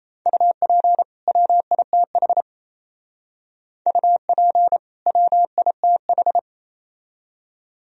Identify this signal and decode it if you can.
Morse code, heard as a keyed tone, about 36 wpm, 700 Hz. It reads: UP WITH UP WITH